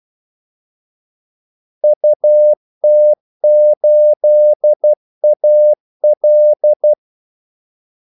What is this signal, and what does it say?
Morse code, heard as a keyed tone, about 12 wpm, 600 Hz. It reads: UT8AL
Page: 2 ekstra ordmellemrum mellem kaldesignalerne